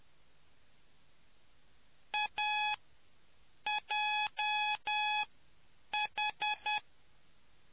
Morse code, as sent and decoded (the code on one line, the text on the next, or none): .- .--- ....
AJH